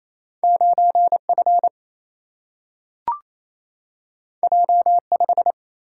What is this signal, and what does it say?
Morse code, heard as a keyed tone, about 28 wpm, 700 Hz. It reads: 9 F J 5
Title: Single-Letter Single-Number 28wpm